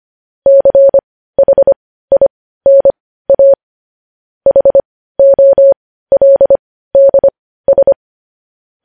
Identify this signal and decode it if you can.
Morse code, heard as a keyed tone, about 25 wpm, spaced as for 15 wpm, 550 Hz. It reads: CHINA HOLDS
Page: news headlines in morse code at 15 wpm